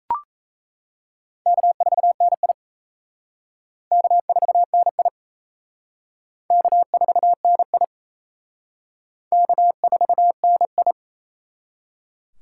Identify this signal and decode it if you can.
Morse code, heard as a keyed tone, about 36 wpm, 700 Hz. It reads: K4NI K4NI K4NI K4NI